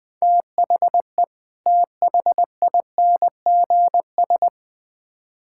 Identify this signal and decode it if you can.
Morse code, heard as a keyed tone, about 20 wpm, 700 Hz. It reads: THE THINGS
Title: Build A Phrase - 3-Word Phrases - 20wpm